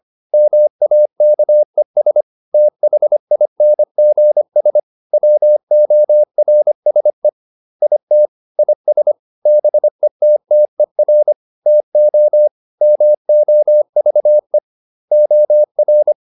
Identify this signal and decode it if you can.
Morse code, heard as a keyed tone, about 25 wpm, 600 Hz. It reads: MAKES THINGS WORSE IT IS BETTER TO MOVE OR